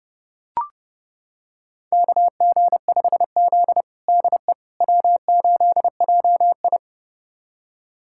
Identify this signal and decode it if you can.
Morse code, heard as a keyed tone, about 30 wpm, 700 Hz. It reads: KG5Z DE W8JI
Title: USA Callsign DE Callsign - 4 Characters - 30wpm